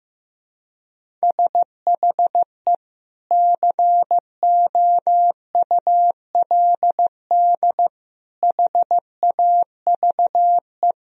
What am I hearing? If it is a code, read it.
Morse code, heard as a keyed tone, about 15 wpm, 700 Hz. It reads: SHE COULD HAVE
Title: Sentences from Top 1000 Words - Review Part 1 - 15wpm